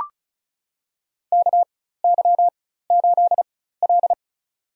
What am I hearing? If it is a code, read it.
Morse code, heard as a keyed tone, about 34 wpm, spaced as for 16 wpm, 700 Hz. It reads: KY8L
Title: USA Call Signs - 4 Characters - 4x Character Spacing - 35wpm